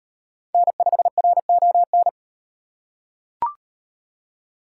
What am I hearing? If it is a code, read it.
Morse code, heard as a keyed tone, about 35 wpm, 700 Hz. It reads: N5RON